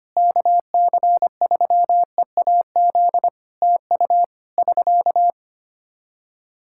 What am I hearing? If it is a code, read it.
Morse code, heard as a keyed tone, about 25 wpm, 700 Hz. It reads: KC3EAZ TU <SK>
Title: Practice QSO - Exchange 4 - Ending - 25wpm